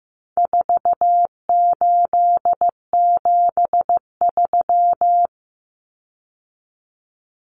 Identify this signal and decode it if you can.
Morse code, heard as a keyed tone, about 15 wpm, 700 Hz. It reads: 4873
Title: Common Cut Numbers - 4-Digit 15wpm